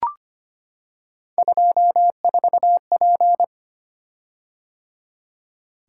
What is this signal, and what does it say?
Morse code, heard as a keyed tone, about 25 wpm, 700 Hz. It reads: 24P
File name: Three Letter-Number Combinations - 25wpm